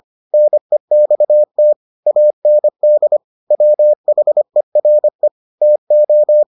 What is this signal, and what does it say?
Morse code, heard as a keyed tone, about 25 wpm, 600 Hz. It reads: NEXT AND WHERE TO